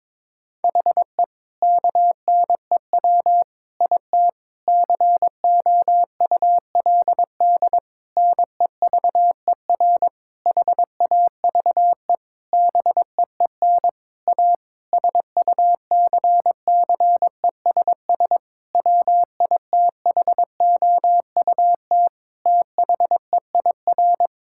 Morse code, heard as a keyed tone, about 22 wpm, 700 Hz. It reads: HE KNEW IT COULD NEVER HAVE BEEN A SUCCESS WITHOUT THEIR
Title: Sentences from Top 1000 Words - Review Part 6 - 22wpm